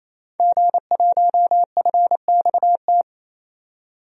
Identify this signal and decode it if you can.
Morse code, heard as a keyed tone, about 28 wpm, 700 Hz. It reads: G1FXT
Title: DXCC-100 Call Signs - 5 Characters - 28wpm